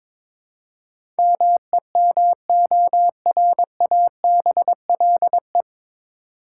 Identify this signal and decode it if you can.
Morse code, heard as a keyed tone, about 22 wpm, 700 Hz. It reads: MEMORABLE
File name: 9 Letter Words - First 3 Letters Given - 22wpm